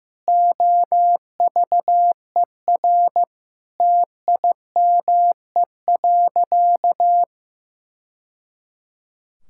Morse code, heard as a keyed tone, about 15 wpm, 700 Hz. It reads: OVER TIME.
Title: Sentences from Top 1000 Words - Review Part 5 - 15wpm